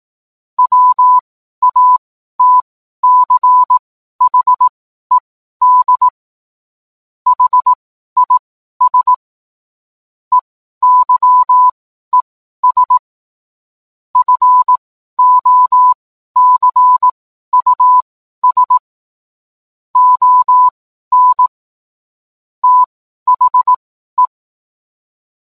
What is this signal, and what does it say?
Morse code, heard as a keyed tone, about 18 wpm, spaced as for 12 wpm, 1000 Hz. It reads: WATCHED HIS EYES FOCUS ON THE